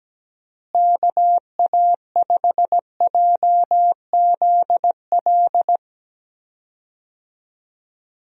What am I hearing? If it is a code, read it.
Morse code, heard as a keyed tone, about 17 wpm, 700 Hz. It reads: KA5JZL
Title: USA Call Signs - 6 Characters - Repeated 2x - 17wpm